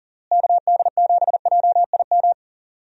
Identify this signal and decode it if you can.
Morse code, heard as a keyed tone, about 40 wpm, 700 Hz. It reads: KD7JIM
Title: USA Call Signs - 6 Characters - Repeated 3x - 40wpm